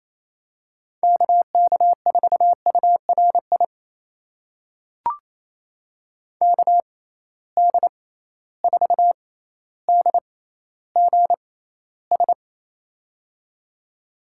Morse code, heard as a keyed tone, about 28 wpm, spaced as for 14 wpm, 700 Hz. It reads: KK4URI KD4DGS
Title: USA Call Signs - 6 Characters - 6x Character Spacing - 28wpm